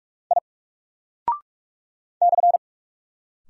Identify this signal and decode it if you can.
Morse code, heard as a keyed tone, about 46 wpm, 700 Hz. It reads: I /
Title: Single Character (Letter:Number:Punctuation:Prosign) Alt 45wpm